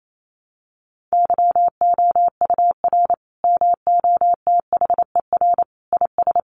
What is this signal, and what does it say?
Morse code, heard as a keyed tone, about 28 wpm, 700 Hz. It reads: YOUR MOTHER IS
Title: Sentences from Top 1100 Words 28wpm